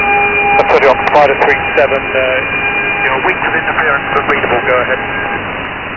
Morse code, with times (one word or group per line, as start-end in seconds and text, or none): none